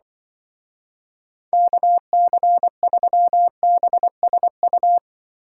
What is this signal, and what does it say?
Morse code, heard as a keyed tone, about 25 wpm, 700 Hz. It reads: KC3BSU
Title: Speed Racing - USA 2x3 Call Signs 20wpm